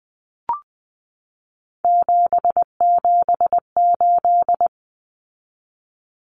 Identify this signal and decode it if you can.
Morse code, heard as a keyed tone, about 20 wpm, 700 Hz. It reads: 778
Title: Numbers - 3-Digit 20wpm